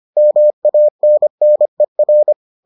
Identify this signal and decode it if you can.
Morse code, heard as a keyed tone, about 25 wpm, 600 Hz. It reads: MANNER